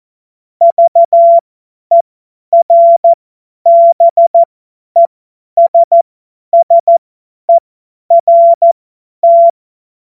Morse code, heard as a keyed tone, about 14 wpm, spaced as for 10 wpm, 680 Hz. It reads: VERBESSERT
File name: Witz-30-September-2025-50-F70